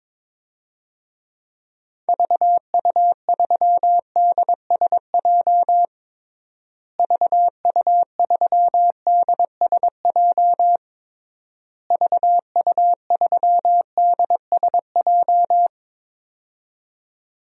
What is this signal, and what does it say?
Morse code, heard as a keyed tone, about 22 wpm, 700 Hz. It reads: VU3DSJ VU3DSJ VU3DSJ
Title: DXCC-100 Call Signs - 6 Characters - Repeated 3x - 22wpm